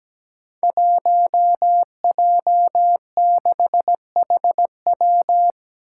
Morse code, heard as a keyed tone, about 17 wpm, 700 Hz. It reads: 1J6HW
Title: Five Challenging Letter-Number Combinations 17wpm